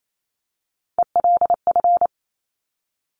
Morse code, heard as a keyed tone, about 28 wpm, 700 Hz. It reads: ELF
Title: Top Three Letter Combinations in Words 28wpm